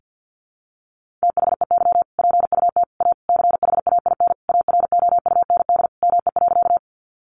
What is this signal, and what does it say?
Morse code, heard as a keyed tone, about 50 wpm, 700 Hz. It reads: THEY PUT A CHAIN AROUND ME.
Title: Sentences from Top 1500 Words 50wpm